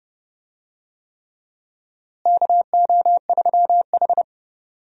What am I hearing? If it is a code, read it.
Morse code, heard as a keyed tone, about 32 wpm, 700 Hz. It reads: KO3H
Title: USA Call Signs - 4 Characters - Repeated 3x - 30wpm